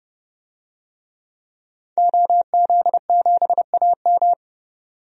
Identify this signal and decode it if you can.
Morse code, heard as a keyed tone, about 30 wpm, 700 Hz. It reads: OZ7AM
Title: DXCC-100 Call Signs - 5 Characters - 30wpm